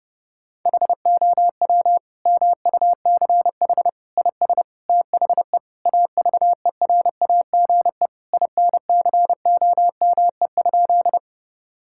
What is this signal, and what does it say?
Morse code, heard as a keyed tone, about 30 wpm, 700 Hz. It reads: HOW MUCH IS THE AVERAGE INCOME?
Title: Sentences from Top 1000 Words - Review Part 6 - 30wpm